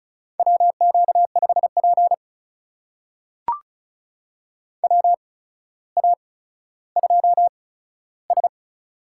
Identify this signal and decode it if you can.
Morse code, heard as a keyed tone, about 36 wpm, spaced as for 15 wpm, 700 Hz. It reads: WQ5P WA2S
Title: USA Call Signs - 4 Characters - 8x Character Spacing - 35wpm